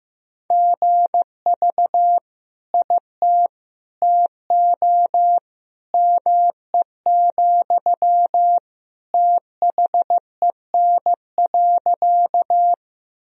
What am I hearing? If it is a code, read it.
Morse code, heard as a keyed tone, about 15 wpm, 700 Hz. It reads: GV IT TO ME, THEN.
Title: Sentences from Top 200 Words and Abbreviations 15wpm